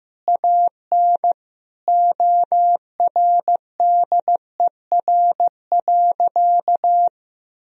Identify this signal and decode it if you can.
Morse code, heard as a keyed tone, about 15 wpm, 700 Hz. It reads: AN ORDER.
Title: Sentences from Top 1000 Words - Review Part 2 - 15wpm